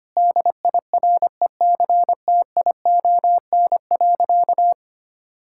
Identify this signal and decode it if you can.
Morse code, heard as a keyed tone, about 25 wpm, 700 Hz. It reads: DIRECTION.
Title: Sentences from Top 1000 Words - Review Part 5 - 25wpm